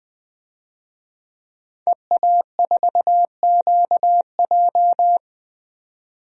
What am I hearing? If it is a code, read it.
Morse code, heard as a keyed tone, about 20 wpm, 700 Hz. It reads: EA4QJ
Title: DXCC-100 Call Signs - 5 Characters - 20wpm